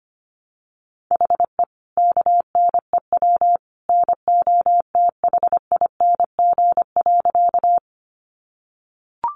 Morse code, heard as a keyed tone, about 25 wpm, 700 Hz. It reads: HE KNEW NOTHING.
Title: Sentences from Top 500 Words 25wpm